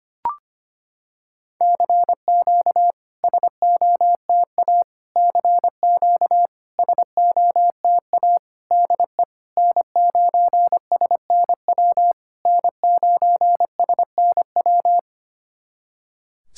Morse code, heard as a keyed tone, about 25 wpm, 700 Hz. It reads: CQ SOTA CQ SOTA DE N9SNW N9SNW
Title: SOTA QSO Practice 25wpm